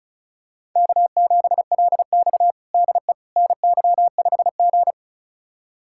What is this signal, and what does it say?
Morse code, heard as a keyed tone, about 36 wpm, 700 Hz. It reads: K7LX DE NY5G
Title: USA Callsign DE Callsign - 4 Characters - 35wpm